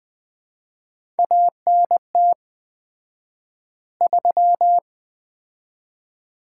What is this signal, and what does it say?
Morse code, heard as a keyed tone, about 20 wpm, 700 Hz. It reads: ANT 3
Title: Common QSO Phrases - 4x Word-Spacing - 20wpm